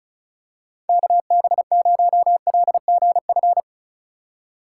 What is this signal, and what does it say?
Morse code, heard as a keyed tone, about 36 wpm, 700 Hz. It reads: KB0LGF
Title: K1USN SST QSO Practice - 35wpm